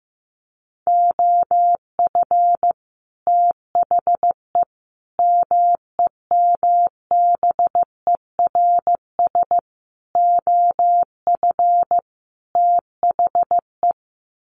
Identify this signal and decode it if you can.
Morse code, heard as a keyed tone, about 15 wpm, 700 Hz. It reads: OF THE MEMBERS OF THE
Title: Sets of 5 Words Encore2 15wpm